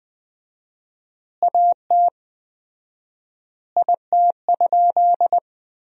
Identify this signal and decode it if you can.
Morse code, heard as a keyed tone, about 21 wpm, 700 Hz. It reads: AT IT?
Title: 4x Word-Spacing - Sentences from Top 100 Words - 20wpm